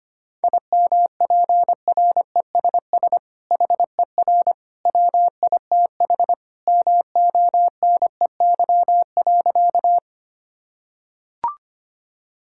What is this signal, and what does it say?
Morse code, heard as a keyed tone, about 25 wpm, 700 Hz. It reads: IMPRESS HER WITH MONEY.
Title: Sentences from Top 1600 Words 25wpm